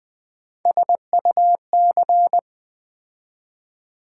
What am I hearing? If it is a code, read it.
Morse code, heard as a keyed tone, about 20 wpm, 700 Hz. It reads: SUC
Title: Build A Word - 3 Letters At A Time - 9-Letter Words - 20wpm